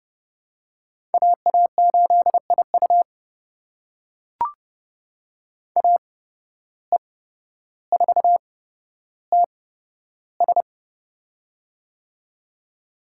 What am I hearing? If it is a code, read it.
Morse code, heard as a keyed tone, about 30 wpm, spaced as for 13 wpm, 700 Hz. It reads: AA8IU AE4TS
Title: USA Call Signs - 5 Characters - 8x Character Spacing - 30wpm